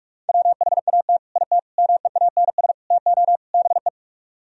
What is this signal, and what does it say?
Morse code, heard as a keyed tone, about 48 wpm, 700 Hz. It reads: WHAT IT MEANS TO BE